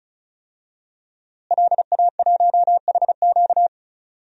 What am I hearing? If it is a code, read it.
Morse code, heard as a keyed tone, about 36 wpm, 700 Hz. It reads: LA1HQ
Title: DXCC-100 Call Signs - 5 Characters - Repeated 3x - 35wpm